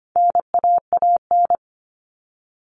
Morse code, heard as a keyed tone, about 25 wpm, 700 Hz. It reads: NAAN
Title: All Cut Numbers - 4-Digit 25wpm